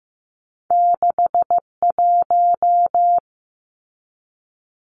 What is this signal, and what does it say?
Morse code, heard as a keyed tone, about 15 wpm, 700 Hz. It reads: 61
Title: Numbers - 2-Digit 15wpm